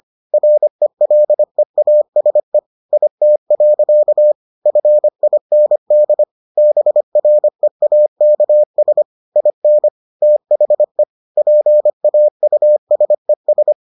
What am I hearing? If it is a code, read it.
Morse code, heard as a keyed tone, about 25 wpm, 600 Hz. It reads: RELEASE IT. FIND BREAKS IN THE PAUSES